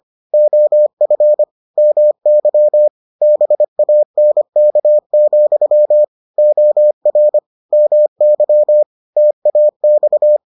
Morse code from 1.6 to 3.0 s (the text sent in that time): MY